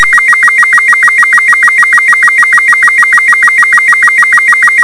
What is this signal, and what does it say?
SSTV (slow-scan television), Robot 36.